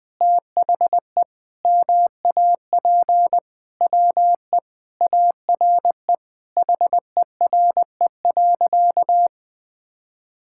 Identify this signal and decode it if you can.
Morse code, heard as a keyed tone, about 20 wpm, 700 Hz. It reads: THE MAP WE ARE HERE.
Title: Sentences with 5-Word Phrases - Spoken Beginning - 20wpm